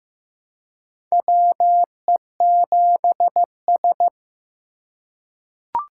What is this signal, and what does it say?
Morse code, heard as a keyed tone, about 15 wpm, 700 Hz. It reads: WE7S
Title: USA Call Signs - 4 Characters - 3x Character Spacing - 15wpm